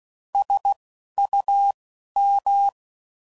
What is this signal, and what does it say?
Morse code, heard as a keyed tone, about 16 wpm, 800 Hz. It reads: S U M